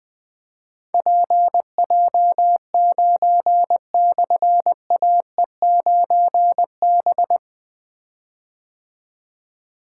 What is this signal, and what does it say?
Morse code, heard as a keyed tone, about 21 wpm, 700 Hz. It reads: PJ9/AE9B
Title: Difficult Call Signs 20wpm